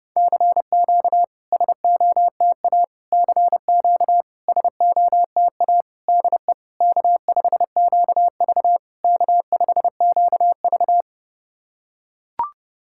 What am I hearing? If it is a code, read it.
Morse code, heard as a keyed tone, about 30 wpm, 700 Hz. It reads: CQ SOTA CQ SOTA DE K5QV K5QV
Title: SOTA QSO Practice 30wpm